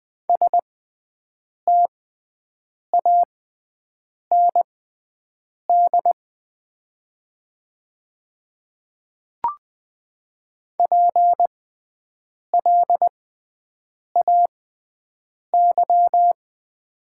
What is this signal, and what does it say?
Morse code, heard as a keyed tone, about 20 wpm, spaced as for 7 wpm, 700 Hz. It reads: STAND PLAY
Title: 6x Character Spacing - Top 300 Words - 20wpm